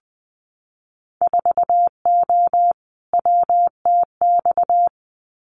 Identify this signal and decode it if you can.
Morse code, heard as a keyed tone, about 20 wpm, 700 Hz. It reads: 4O WTX
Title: Winter Field Day Exchanges Encore 20wpm